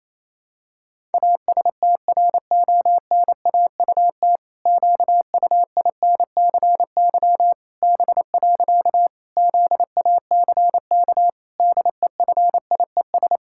Morse code, heard as a keyed tone, about 28 wpm, 700 Hz. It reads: ASTRONAUT QUINCY B. ZACK DEFIES
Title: Pangram Challenge v2 - 28wpm